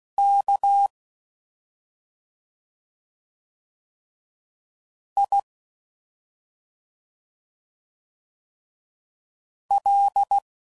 Morse code, heard as a keyed tone, about 16 wpm, 800 Hz. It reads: K I L